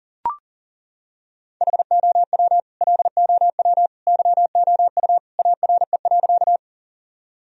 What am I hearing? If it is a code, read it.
Morse code, heard as a keyed tone, about 37 wpm, 700 Hz. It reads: HOW LOW YOU ARE.
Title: Sentences from Top 1000 Words - Review Part 1 - 40wpm